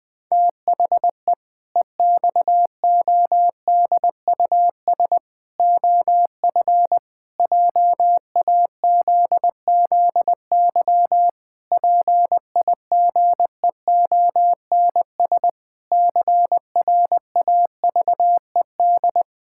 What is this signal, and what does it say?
Morse code, heard as a keyed tone, about 20 wpm, 700 Hz. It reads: THE EXODUS OF JAZZY PIGEONS CRAVED
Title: Pangram Challenge v2 - 20wpm